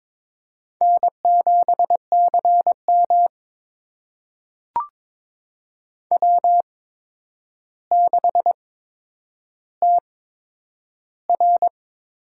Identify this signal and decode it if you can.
Morse code, heard as a keyed tone, about 22 wpm, spaced as for 9 wpm, 700 Hz. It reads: N7CM W6TR
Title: USA Call Signs - 8x Character Spacing - 22wpm